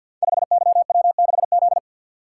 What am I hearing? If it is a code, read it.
Morse code, heard as a keyed tone, about 52 wpm, 700 Hz. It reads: 5YW6Z
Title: Code Groups - 5 Letter:Numbers - 50wpm